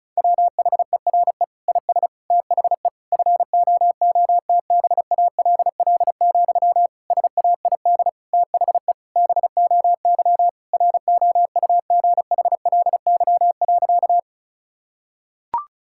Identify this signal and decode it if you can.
Morse code, heard as a keyed tone, about 36 wpm, 700 Hz. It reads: WHERE IS THE FOOTBALL, SAID THE BOY ROUGHLY.
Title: Sentences from Top 1600 Words 35wpm